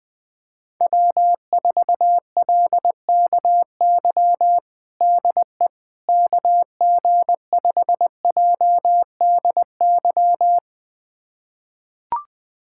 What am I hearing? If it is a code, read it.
Morse code, heard as a keyed tone, about 21 wpm, 700 Hz. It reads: W4LKY DE KG5JDY
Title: USA Callsign DE Callsign 20wpm